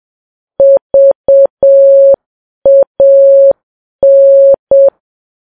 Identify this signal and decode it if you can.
Morse code, heard as a keyed tone, about 7 wpm, 550 Hz. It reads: VAN
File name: Files Web tls / Morse on9fv / morse7kps